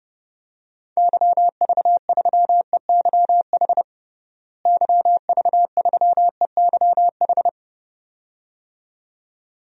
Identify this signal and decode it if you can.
Morse code, heard as a keyed tone, about 30 wpm, 700 Hz. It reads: YV3EYH YV3EYH
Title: DXCC-100 Call Signs - 6 Characters - Repeated 2x - 30wpm